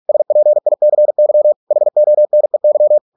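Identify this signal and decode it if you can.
Morse code, heard as a keyed tone, about 46 wpm, 600 Hz. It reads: SPICY HONEY
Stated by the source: Morse practice at 45 wpm, the novel Never by Ken Follett in English, from CW med Gnister.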